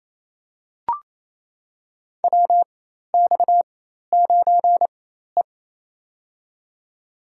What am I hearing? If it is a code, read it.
Morse code, heard as a keyed tone, about 28 wpm, spaced as for 13 wpm, 700 Hz. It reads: WX9E
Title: USA Call Signs - 4 Characters - 4x Character Spacing - 28wpm